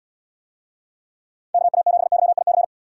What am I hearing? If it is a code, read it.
Morse code, heard as a keyed tone, about 78 wpm, 700 Hz. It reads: KA7YIC